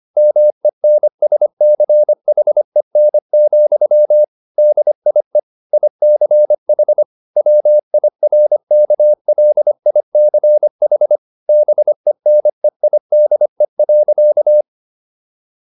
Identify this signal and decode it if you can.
Morse code, heard as a keyed tone, about 25 wpm, 600 Hz. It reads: MENSCHEN, DIE ICH WIRKLICH BENEIDE.